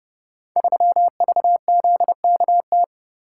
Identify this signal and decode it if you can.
Morse code, heard as a keyed tone, about 30 wpm, 700 Hz. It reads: 3VZKT
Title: Code Groups - 5 Letter:Numbers - 30wpm